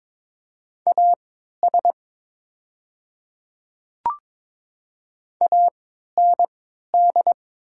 Morse code, heard as a keyed tone, about 22 wpm, spaced as for 12 wpm, 700 Hz. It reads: AS AND
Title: 3x Character Spacing - Top 100 Words - 22wpm